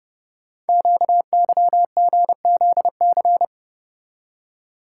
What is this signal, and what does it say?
Morse code, heard as a keyed tone, about 30 wpm, 700 Hz. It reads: QYGZC
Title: Code Groups - 5 Letters - 30wpm